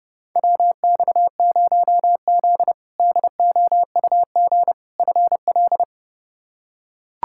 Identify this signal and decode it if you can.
Morse code, heard as a keyed tone, about 30 wpm, 700 Hz. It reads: WX0Z DOUG FL
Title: K1USN SST QSO Practice - 30wpm